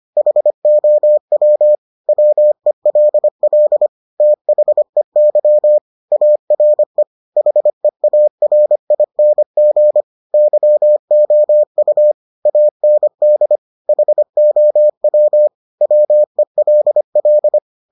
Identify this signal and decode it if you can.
Morse code, heard as a keyed tone, about 25 wpm, 600 Hz. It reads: HOW WELL THEY ARE HEARING YOU AND HOW WELL